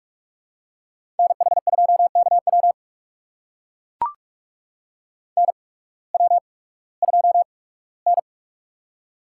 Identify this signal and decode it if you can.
Morse code, heard as a keyed tone, about 44 wpm, spaced as for 21 wpm, 700 Hz. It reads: NH2KW NW2N
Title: USA Call Signs - 8x Character Spacing - 45wpm